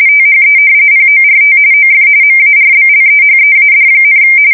RTTY, about 45.45 baud. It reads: THE PORT OF MARBLEHEAD,
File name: fsk170 / usnavy1